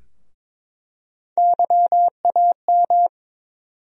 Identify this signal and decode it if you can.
Morse code, heard as a keyed tone, about 22 wpm, 700 Hz. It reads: YAM
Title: USA Call Sign Postfixes - 3 Letters 22wpm